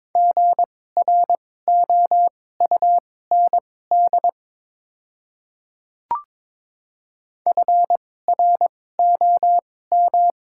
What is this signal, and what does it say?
Morse code, heard as a keyed tone, about 22 wpm, spaced as for 16 wpm, 700 Hz. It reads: GROUND FROM
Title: 2x Character Spacing - Top 500 Words - 22wpm